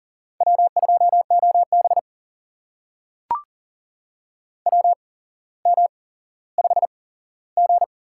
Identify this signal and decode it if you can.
Morse code, heard as a keyed tone, about 40 wpm, spaced as for 17 wpm, 700 Hz. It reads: W2OB WM5G
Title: USA Call Signs - 4 Characters - 8x Character Spacing - 40wpm